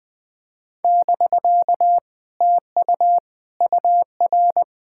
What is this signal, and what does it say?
Morse code, heard as a keyed tone, about 20 wpm, 700 Hz. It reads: <BK> TU UR
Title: Speed-Racing Common QSO Phrases - 20wpm